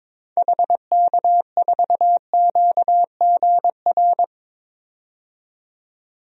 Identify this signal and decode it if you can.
Morse code, heard as a keyed tone, about 22 wpm, 700 Hz. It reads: HK4QGR
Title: DXCC-100 Call Signs - 6 Characters - 22wpm